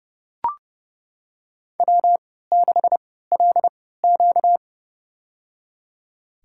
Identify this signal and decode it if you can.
Morse code, heard as a keyed tone, about 30 wpm, spaced as for 17 wpm, 700 Hz. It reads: W6LQ
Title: USA Call Signs - 4 Characters - 3x Character Spacing - 30wpm